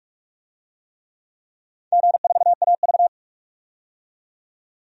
Morse code, heard as a keyed tone, about 46 wpm, 690 Hz. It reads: G4AV
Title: DXCC-100 Call Signs - 4 Characters - 45wpm